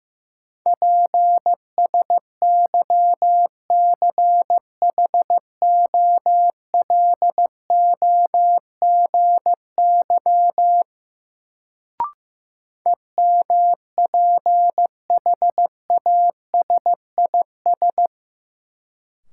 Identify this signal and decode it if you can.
Morse code, heard as a keyed tone, about 16 wpm, 700 Hz. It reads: PSYCHOLOGY EMPHASIS
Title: Sentences from Top 1400 Words - New Word Drill - 15wpm